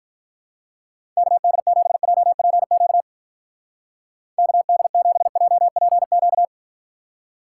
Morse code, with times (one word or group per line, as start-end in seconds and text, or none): 1.2-3.0 KD7JPQ
4.4-6.5 KD7JPQ